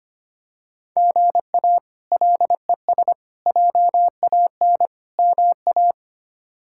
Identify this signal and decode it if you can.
Morse code, heard as a keyed tone, about 25 wpm, 700 Hz. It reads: GA LES JAN MA
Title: K1USN SST QSO Practice - 25wpm